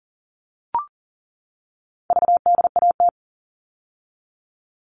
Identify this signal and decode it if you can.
Morse code, heard as a keyed tone, about 40 wpm, 700 Hz. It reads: VDAT
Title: All Cut Numbers - 4-Digit 40wpm